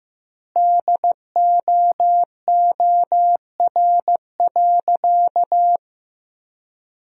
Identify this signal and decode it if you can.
Morse code, heard as a keyed tone, about 15 wpm, 700 Hz. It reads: DOOR.
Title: 2x Word-Spacing - Sentences from Top 400 Words - 15wpm